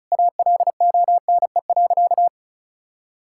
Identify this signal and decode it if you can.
Morse code, heard as a keyed tone, about 34 wpm, 700 Hz. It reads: ALONE.
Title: Sentences from Top 1000 Words - Review Part 6 - 35wpm